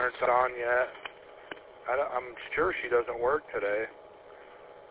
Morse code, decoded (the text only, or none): none